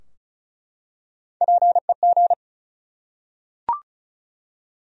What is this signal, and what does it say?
Morse code, heard as a keyed tone, about 36 wpm, 700 Hz. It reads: PEG
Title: USA Common Names - Max 5 Letters - 35wpm